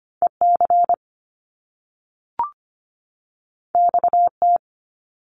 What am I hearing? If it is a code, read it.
Morse code, heard as a keyed tone, about 25 wpm, 700 Hz. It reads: E C X T
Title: Top Two Letter Combinations in Words 25wpm